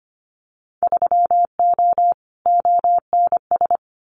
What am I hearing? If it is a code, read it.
Morse code, heard as a keyed tone, about 25 wpm, 700 Hz. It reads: 3O ONS
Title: Winter Field Day Exchanges 25wpm